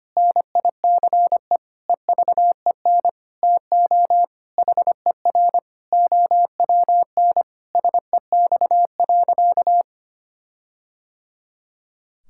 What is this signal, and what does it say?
Morse code, heard as a keyed tone, about 25 wpm, 700 Hz. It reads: NICE EVEN TO HER OWN SEX.
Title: Sentences from Top 1600 Words 25wpm